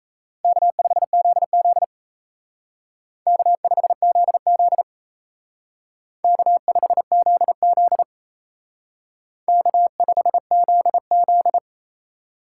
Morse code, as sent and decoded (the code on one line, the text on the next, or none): -.- ..... --.. --.. / -.- ..... --.. --.. / -.- ..... --.. --.. / -.- ..... --.. --..
K5ZZ K5ZZ K5ZZ K5ZZ